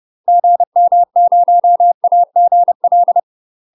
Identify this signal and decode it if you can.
Morse code, heard as a keyed tone, about 30 wpm, 700 Hz. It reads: GM0AGL
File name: UK MASTER QSOs TRAINING / 30WPM / UK MASTER QSOs 0051